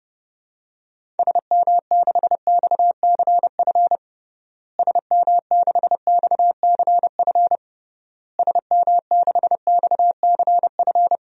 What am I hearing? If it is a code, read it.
Morse code, heard as a keyed tone, about 30 wpm, 700 Hz. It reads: SM6XCF SM6XCF SM6XCF
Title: DXCC-100 Call Signs - 6 Characters - Repeated 3x - 30wpm